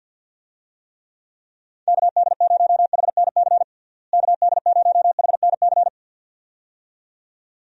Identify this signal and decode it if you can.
Morse code, heard as a keyed tone, about 46 wpm, 700 Hz. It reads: KD0HNC KD0HNC
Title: USA Call Signs - 6 Characters - Repeated 2x - 50wpm